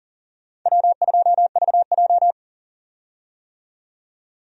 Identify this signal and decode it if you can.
Morse code, heard as a keyed tone, about 40 wpm, 700 Hz. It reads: W2VJ
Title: USA Call Signs - 4 Characters - 40wpm